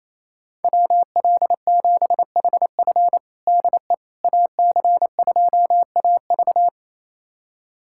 Morse code, heard as a keyed tone, about 28 wpm, 700 Hz. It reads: WL7HF DE AC2AV
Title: USA Callsign DE Callsign - 5 Characters - 28wpm